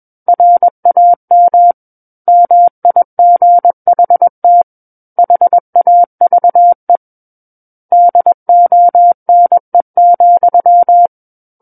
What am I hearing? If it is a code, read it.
Morse code, heard as a keyed tone, about 21 wpm, 700 Hz. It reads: RAM MIGHT HAVE DONE,
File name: Book One / 12-Chapter 12 - 21 WPM